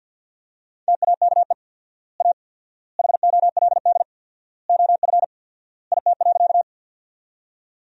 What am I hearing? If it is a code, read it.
Morse code, heard as a keyed tone, about 50 wpm, 700 Hz. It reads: TAKE A HOLD OF IT.
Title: 4x Word-Spacing - Sentences from Top 400 Words - 50wpm